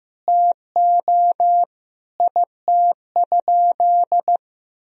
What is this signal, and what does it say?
Morse code, heard as a keyed tone, about 15 wpm, 700 Hz. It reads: TO IT?
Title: Sentences from Top 1000 Words - Review Part 4 - 15wpm